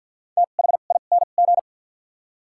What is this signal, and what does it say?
Morse code, heard as a keyed tone, about 53 wpm, spaced as for 37 wpm, 700 Hz. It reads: THING
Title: 2x Character Spacing - Top 500 Words - 50wpm